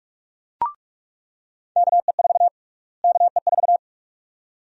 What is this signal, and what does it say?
Morse code, heard as a keyed tone, about 46 wpm, 700 Hz. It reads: KE4 KE4
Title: USA Call Sign Prefixes - Repeated 2x - 45wpm